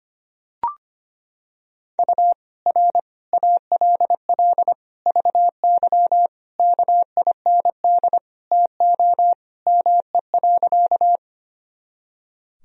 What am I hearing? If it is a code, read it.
Morse code, heard as a keyed tone, about 25 wpm, 700 Hz. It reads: U R ALL VY KIND TO ME.
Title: Sentences from Top 200 Words and Abbreviations 25wpm